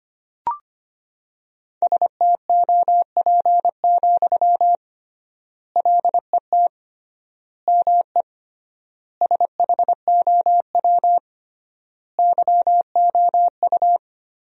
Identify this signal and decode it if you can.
Morse code, heard as a keyed tone, about 25 wpm, 700 Hz. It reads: STOP, LET ME SHOW YOU
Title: 3x Word-Spacing - Sentences from Top 300 Words - 25wpm